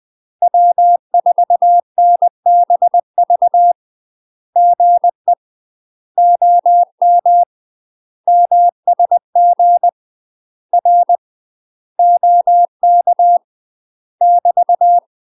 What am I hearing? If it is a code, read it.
Morse code, heard as a keyed tone, about 20 wpm, 700 Hz. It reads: W4NBV GE OM MSG R OK <BT>
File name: UK MASTER QSOs TRAINING / 20WPM / UK MASTER QSOs 0050